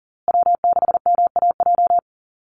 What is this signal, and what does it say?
Morse code, heard as a keyed tone, about 40 wpm, 700 Hz. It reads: W6MAJ
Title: USA Call Signs Repeated 3x 40wpm